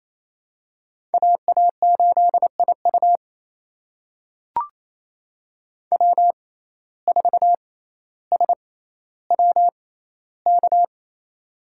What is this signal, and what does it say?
Morse code, heard as a keyed tone, about 28 wpm, spaced as for 14 wpm, 700 Hz. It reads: AA8IU W4SWK
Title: USA Call Signs - 5 Characters - 6x Character Spacing - 28wpm